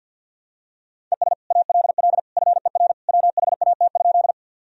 Morse code, heard as a keyed tone, about 50 wpm, 700 Hz. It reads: ES ALL FER WHAT?
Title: Sentences from Top 100 Words and Abbreviations 50wpm